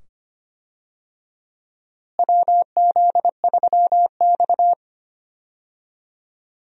Morse code, heard as a keyed tone, about 25 wpm, 700 Hz. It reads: WZ3X
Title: USA Call Signs - 4 Characters - Repeated 2x - 25wpm